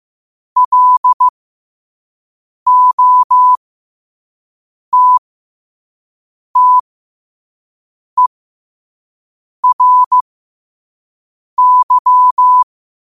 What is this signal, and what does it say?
Morse code, heard as a keyed tone, about 15 wpm, spaced as for 5 wpm, 1000 Hz. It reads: LOTTERY
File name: qotdfiles / 5wpm